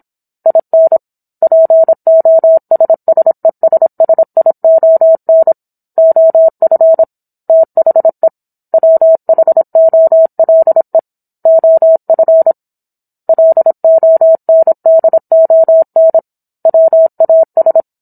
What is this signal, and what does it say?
Morse code, heard as a keyed tone, about 25 wpm, 650 Hz. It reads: IN POSSESSION OF THE WHOLE OF LONDON WAS